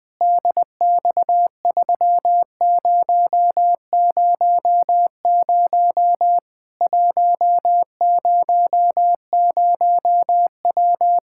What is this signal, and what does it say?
Morse code, heard as a keyed tone, about 21 wpm, 700 Hz. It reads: DX3000 100W
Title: K1USN SST QSO Practice 20wpm